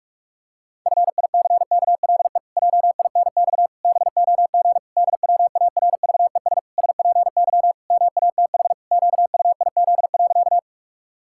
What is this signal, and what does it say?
Morse code, heard as a keyed tone, about 46 wpm, 700 Hz. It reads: FICKLE JINX BOG DWARVES SPY MATH QUIZ.
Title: Pangram Challenge v2 - 45wpm